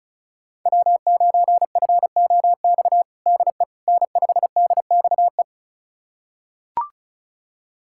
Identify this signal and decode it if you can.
Morse code, heard as a keyed tone, about 34 wpm, 700 Hz. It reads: W9FOX DE N5DXE
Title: USA Callsign DE Callsign - 5 Characters - 35wpm